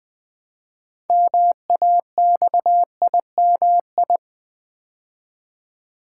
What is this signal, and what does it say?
Morse code, heard as a keyed tone, about 20 wpm, 700 Hz. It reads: MAXIMI
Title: Build A Word - 3 Letters At A Time - 8-Letter Words - 20wpm